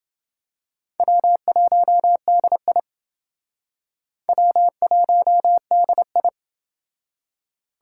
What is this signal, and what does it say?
Morse code, heard as a keyed tone, about 29 wpm, 700 Hz. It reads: W1DI W1DI